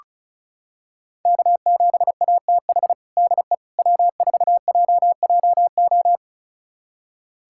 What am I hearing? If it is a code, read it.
Morse code, heard as a keyed tone, about 36 wpm, 700 Hz. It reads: K7ATH DE W4JJO
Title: USA Callsign DE Callsign - 5 Characters - 35wpm